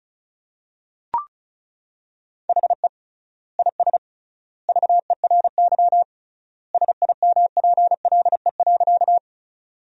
Morse code, heard as a keyed tone, about 34 wpm, 700 Hz. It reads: HE IS VERY SIMPLE.
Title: 3x Word-Spacing - Sentences from Top 400 Words - 35wpm